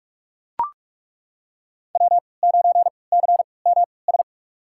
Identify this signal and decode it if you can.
Morse code, heard as a keyed tone, about 42 wpm, spaced as for 25 wpm, 700 Hz. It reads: W9CMS
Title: USA Call Signs - 5 Characters - 3x Character Spacing - 45wpm